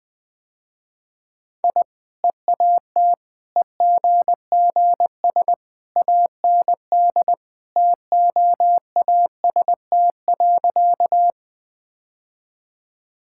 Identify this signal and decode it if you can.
Morse code, heard as a keyed tone, about 21 wpm, 700 Hz. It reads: I EAT EGGS AND TOAST.
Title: Easy Sentences with Hints - 20wpm